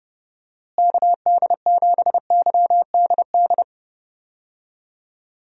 Morse code, heard as a keyed tone, about 30 wpm, 700 Hz. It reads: KD7YDD
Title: USA Call Signs - 6 Characters - Repeated 2x - 30wpm